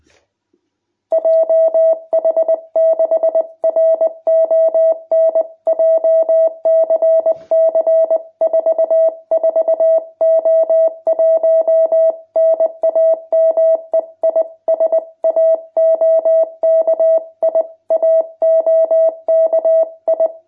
Morse code, heard as a keyed tone, about 20 wpm, 660 Hz. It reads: JH6RON JCC 44O1 NAME IS AOKI AOKI
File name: jh8cbh / 2021cw